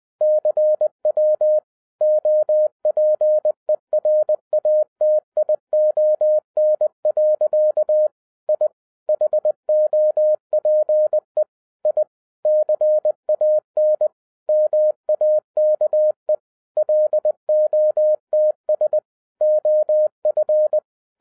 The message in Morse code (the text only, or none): CW OPERATION. I HOPE I CAN MAKE LOTS OF